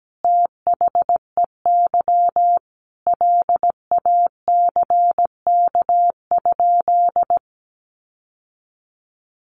Morse code, heard as a keyed tone, about 17 wpm, 700 Hz. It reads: THEY LACK?
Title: Sentences from Top 1100 Words 17wpm